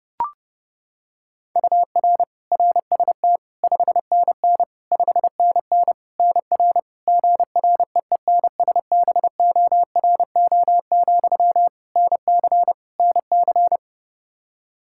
Morse code, heard as a keyed tone, about 30 wpm, 700 Hz. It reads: UR RST 5NN 5NN NR GREENSBORO, NC NC
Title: Practice QSO - Exchange 1 - Beginning - 30wpm